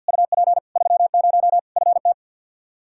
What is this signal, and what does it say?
Morse code, heard as a keyed tone, about 47 wpm, 700 Hz. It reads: UP 30 FT
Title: Practice QSO - Exchange 2 - Info - 50wpm